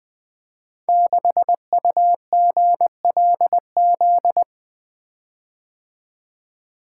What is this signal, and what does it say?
Morse code, heard as a keyed tone, about 20 wpm, 700 Hz. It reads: 6UGLZ
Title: Code Groups - 5 Letter:Numbers - 20wpm